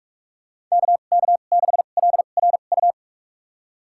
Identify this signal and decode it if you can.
Morse code, heard as a keyed tone, about 46 wpm, spaced as for 32 wpm, 700 Hz. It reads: KK6LRU
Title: USA Call Signs - 6 Characters - 2x Character Spacing - 45wpm